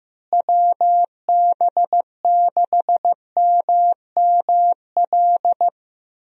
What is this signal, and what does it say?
Morse code, heard as a keyed tone, about 15 wpm, 700 Hz. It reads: WB6MML
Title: USA Call Signs - 6 Characters - Repeated 3x - 15wpm